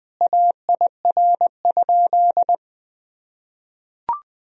Morse code, heard as a keyed tone, about 20 wpm, 700 Hz. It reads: AIR?
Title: Sentences from Top 1000 Words - Review Part 1 - 20wpm